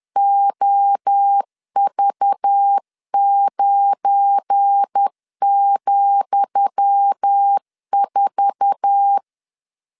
Morse code, heard as a keyed tone, about 10 wpm, 800 Hz. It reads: OV9,4
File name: LEZIONI / 10 SIGNS / 10D COMA 10wpm 3013